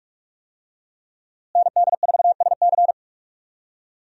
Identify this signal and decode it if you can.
Morse code, heard as a keyed tone, about 41 wpm, 700 Hz. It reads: ND4SC
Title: USA Call Signs - 5 Characters - Repeated 2x - 45wpm